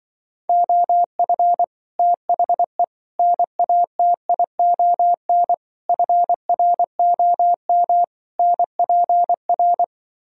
Morse code, heard as a keyed tone, about 24 wpm, 700 Hz. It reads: OF THE NATION FROM NPR
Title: Speed-Racing - Sets of 5 Words 20wpm